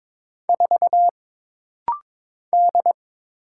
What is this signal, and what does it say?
Morse code, heard as a keyed tone, about 22 wpm, 700 Hz. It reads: 4 D
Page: This rapid-fire timing is geared to push your ICR (Instant Character Recognition) skills.